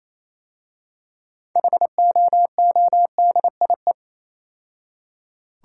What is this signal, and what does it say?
Morse code, heard as a keyed tone, about 27 wpm, 700 Hz. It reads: HOODIE
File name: Easy Words - Speed-Racing with Hints - 22wpm